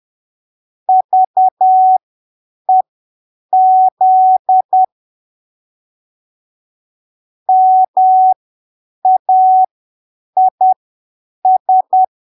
Morse code, heard as a keyed tone, about 10 wpm, spaced as for 7 wpm, 750 Hz. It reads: VEZ MAIS
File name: arquivos / texto / texto 24 10ppm